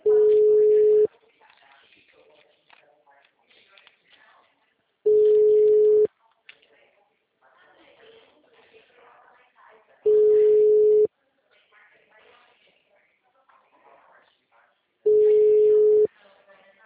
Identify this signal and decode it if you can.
Ringback tone (Europe, 425 Hz), with speech.
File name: out- -2012- - - .36